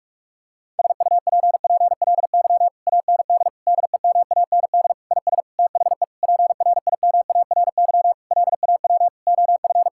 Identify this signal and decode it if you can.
Morse code, heard as a keyed tone, about 46 wpm, 700 Hz. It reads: SUPPLY AND DEMAND IS THE PRIMARY LAW OF
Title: Sentences from Top 1000 Words - Review Part 6 - 45wpm